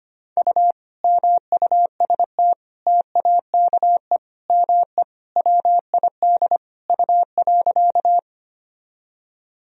Morse code, heard as a keyed tone, about 25 wpm, 700 Hz. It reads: U MUST TAKE ME WID U.
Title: Sentences from Top 200 Words and Abbreviations 25wpm